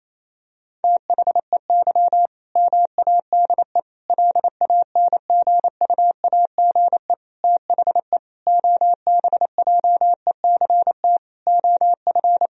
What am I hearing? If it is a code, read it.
Morse code, heard as a keyed tone, about 27 wpm, 700 Hz. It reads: THEY MADE LANGUAGE THE OBJECT OF
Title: Sentences from Top 1000 Words - Review Part 3 - 28wpm